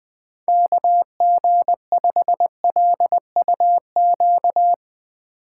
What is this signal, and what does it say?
Morse code, heard as a keyed tone, about 20 wpm, 700 Hz. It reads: KG5LUQ
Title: Speed Racing - USA 2x3 Call Signs 17wpm